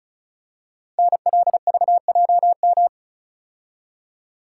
Morse code, heard as a keyed tone, about 36 wpm, 700 Hz. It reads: NLVJM
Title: Code Groups - 5 Letters - 35wpm